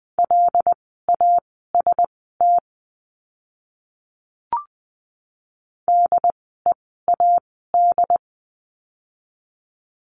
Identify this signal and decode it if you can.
Morse code, heard as a keyed tone, about 20 wpm, spaced as for 14 wpm, 700 Hz. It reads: LAST DEAD
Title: 075 Lesson 13f - Letter L - Words 20wpm